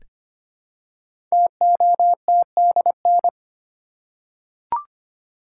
Morse code, heard as a keyed tone, about 25 wpm, 700 Hz. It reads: TOTDN
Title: Code Groups - 5 Letters - 25wpm